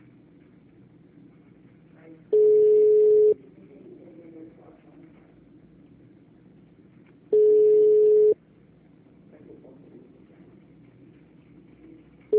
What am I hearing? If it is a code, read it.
Ringback tone (Europe, 425 Hz).